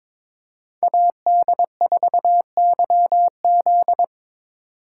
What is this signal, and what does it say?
Morse code, heard as a keyed tone, about 21 wpm, 700 Hz. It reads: AD4YZ
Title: USA Call Signs - 5 Characters - 4x Character Spacing - 22wpm